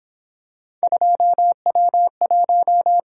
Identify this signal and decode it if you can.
Morse code, heard as a keyed tone, about 26 wpm, 700 Hz. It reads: 2W1
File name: Speed-Racing - Three Letter-Number Combinations - 22wpm